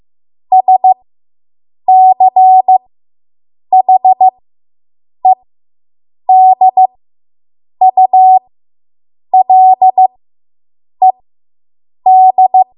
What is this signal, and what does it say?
Morse code, heard as a keyed tone, about 15 wpm, spaced as for 7 wpm, 750 Hz. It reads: SCHEDULED